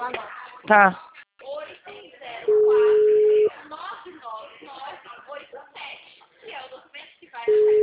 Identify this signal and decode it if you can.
Ringback tone (Europe, 425 Hz), with speech.
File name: out- -2010- - - .55687